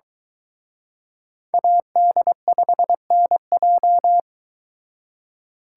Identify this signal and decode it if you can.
Morse code, heard as a keyed tone, about 24 wpm, 700 Hz. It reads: AD5NJ